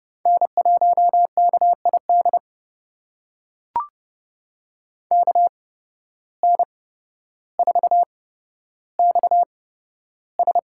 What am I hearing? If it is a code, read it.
Morse code, heard as a keyed tone, about 30 wpm, spaced as for 13 wpm, 700 Hz. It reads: N1KID KN4XS
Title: USA Call Signs - 5 Characters - 8x Character Spacing - 30wpm